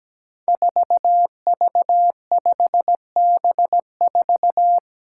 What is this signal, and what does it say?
Morse code, heard as a keyed tone, about 17 wpm, 700 Hz. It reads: 4V5B4
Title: Five Challenging Letter-Number Combinations 17wpm